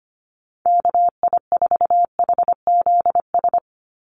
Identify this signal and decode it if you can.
Morse code, heard as a keyed tone, about 25 wpm, 700 Hz. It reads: KI4HZS
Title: USA Call Signs 25wpm